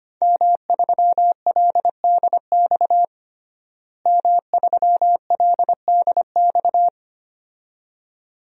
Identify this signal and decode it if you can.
Morse code, heard as a keyed tone, about 25 wpm, 700 Hz. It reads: M3LDX M3LDX
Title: DXCC-100 Call Signs - 5 Characters - Repeated 2x - 25wpm